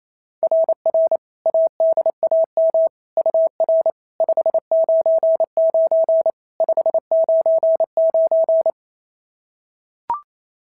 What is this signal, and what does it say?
Morse code, heard as a keyed tone, about 27 wpm, 690 Hz. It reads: RR ADAM UR 599 599
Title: SOTA QSO Practice 28wpm